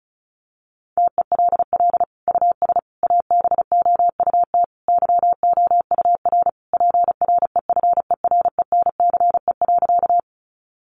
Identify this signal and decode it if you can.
Morse code, heard as a keyed tone, about 34 wpm, 700 Hz. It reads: TELL US ABOUT YOUR PREFERENCE.
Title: Sentences from Top 1500 Words 35wpm